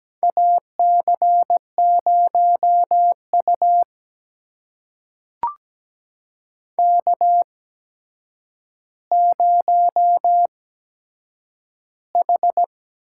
Morse code, heard as a keyed tone, about 17 wpm, spaced as for 8 wpm, 700 Hz. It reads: AC0U K0H
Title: USA Call Signs - 4 Characters - 8x Character Spacing - 17wpm